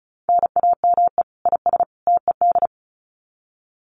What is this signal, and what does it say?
Morse code, heard as a keyed tone, about 34 wpm, 700 Hz. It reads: NAME IS TED
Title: Common QSO Phrases 35wpm